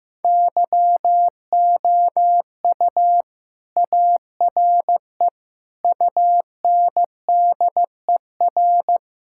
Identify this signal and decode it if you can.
Morse code, heard as a keyed tone, about 15 wpm, 700 Hz. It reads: YOU ARE UNDER
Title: Sentences from Top 1600 Words 15wpm